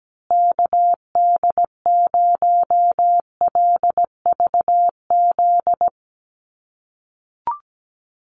Morse code, heard as a keyed tone, about 17 wpm, 700 Hz. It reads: KD0LVZ
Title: USA Call Signs Repeated 3x 17wpm